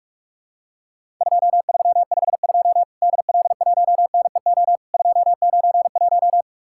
Morse code, heard as a keyed tone, about 48 wpm, 700 Hz. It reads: 2352 DL1NEO 291
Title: Worked All Europe DX Contest - QTC Exchange - 45wpm